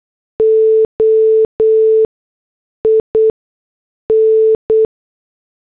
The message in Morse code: OIN